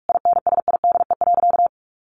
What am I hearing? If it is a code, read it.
Morse code, heard as a keyed tone, about 44 wpm, 700 Hz. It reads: INSIDE.
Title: Additional Easy Phrases 45wpm